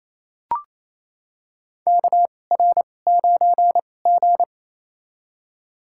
Morse code, heard as a keyed tone, about 28 wpm, spaced as for 20 wpm, 700 Hz. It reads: KR9G
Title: USA Call Signs - 4 Characters - 2x Character Spacing - 28wpm